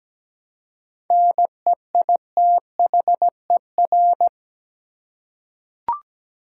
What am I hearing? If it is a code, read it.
Morse code, heard as a keyed tone, about 17 wpm, 700 Hz. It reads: NEITHER
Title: Seven Letter Words - 17wpm